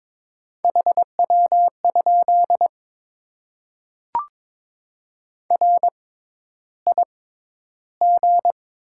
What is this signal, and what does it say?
Morse code, heard as a keyed tone, about 21 wpm, spaced as for 11 wpm, 690 Hz. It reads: HW? RIG
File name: Top 20 QSO Elements - 6x Character Spacing - 22wpm